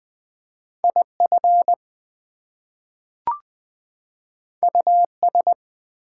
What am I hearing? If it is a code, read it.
Morse code, heard as a keyed tone, about 20 wpm, 700 Hz. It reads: IF US
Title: Two Letter Words - 20wpm